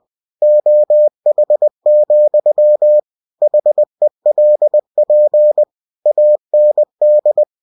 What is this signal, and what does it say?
Morse code, heard as a keyed tone, about 20 wpm, 600 Hz. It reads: OH, HELP AND
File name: cw-books / winnie-the-pooh / winnie-20wpm-page-0016